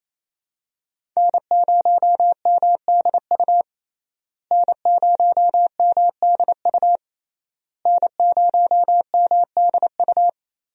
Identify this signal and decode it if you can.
Morse code, heard as a keyed tone, about 28 wpm, 700 Hz. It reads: N0MDU N0MDU N0MDU
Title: USA Call Signs - 5 Characters - Repeated 3x - 28wpm